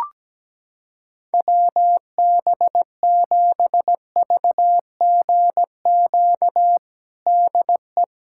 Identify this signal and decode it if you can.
Morse code, heard as a keyed tone, about 17 wpm, 700 Hz. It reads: WB7VGQ DE
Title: USA Callsign DE Callsign - 6 Characters - 17wpm